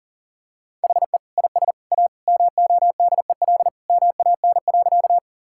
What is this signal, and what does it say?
Morse code, heard as a keyed tone, about 40 wpm, 700 Hz. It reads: HE IS A MODEL MAN.
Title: Sentences from Top 1000 Words - Review Part 5 - 40wpm